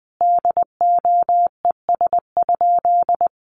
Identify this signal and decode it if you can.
Morse code, heard as a keyed tone, about 20 wpm, 700 Hz. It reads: DOES?
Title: Sentences from Top 1300 Words 20wpm